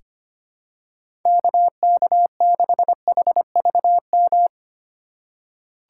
Morse code, heard as a keyed tone, about 25 wpm, 700 Hz. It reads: KK6HVM
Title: USA Call Signs - 6 Characters - 3x Character Spacing - 25wpm